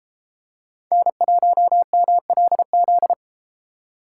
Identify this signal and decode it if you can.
Morse code, heard as a keyed tone, about 30 wpm, 700 Hz. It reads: N1MLZ